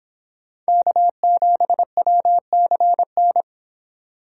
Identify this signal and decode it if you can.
Morse code, heard as a keyed tone, about 26 wpm, 700 Hz. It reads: K7WCN